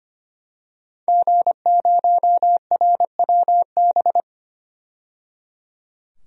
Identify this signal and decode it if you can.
Morse code, heard as a keyed tone, about 25 wpm, 700 Hz. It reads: G0RWB
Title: DXCC-100 Call Signs - 5 Characters - 25wpm